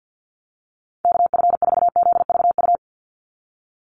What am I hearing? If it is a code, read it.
Morse code, heard as a keyed tone, about 50 wpm, 700 Hz. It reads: KF4ZVU
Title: USA Call Signs 50wpm